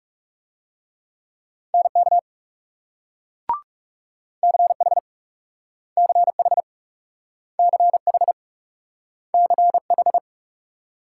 Morse code, heard as a keyed tone, about 38 wpm, 700 Hz. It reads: NK CH CH CH CH